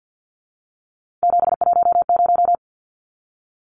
Morse code, heard as a keyed tone, about 50 wpm, 700 Hz. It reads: 710
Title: Numbers - 3-Digit 50wpm